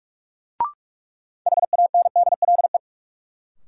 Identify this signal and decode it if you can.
Morse code, heard as a keyed tone, about 42 wpm, 700 Hz. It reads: HANDLE
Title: Sentences from Top 1100 Words - New Word Drill - 45wpm